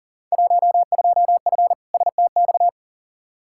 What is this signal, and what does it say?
Morse code, heard as a keyed tone, about 37 wpm, 700 Hz. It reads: 12F STX
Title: Field Day Exchanges Encore - 40wpm